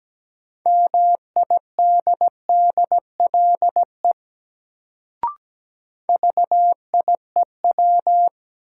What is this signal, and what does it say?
Morse code, heard as a keyed tone, about 17 wpm, 700 Hz. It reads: MIDDLE VIEW
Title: Sentences from Top 900 Words - New Word Drill - 17wpm